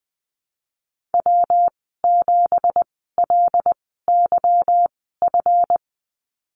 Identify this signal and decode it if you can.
Morse code, heard as a keyed tone, about 20 wpm, 700 Hz. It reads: W 7 L Y F
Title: 207 Lesson 33g - Letter j - Call Signs 20wpm